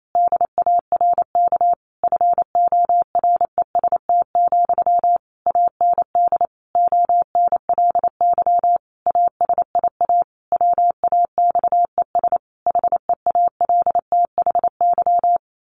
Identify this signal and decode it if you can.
Morse code, heard as a keyed tone, about 28 wpm, 700 Hz. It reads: DARK FOREST, AND ONLY ASIA WAXES HEALTHY
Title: A Sound Of Thunder 28wpm